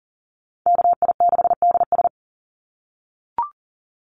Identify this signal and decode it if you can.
Morse code, heard as a keyed tone, about 38 wpm, 700 Hz. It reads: KI6DS
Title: NA SOTA Chaser Callsigns 40wpm